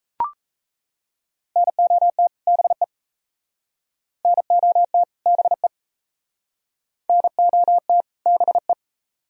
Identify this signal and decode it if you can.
Morse code, heard as a keyed tone, about 37 wpm, 700 Hz. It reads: NOT BE NOT BE NOT BE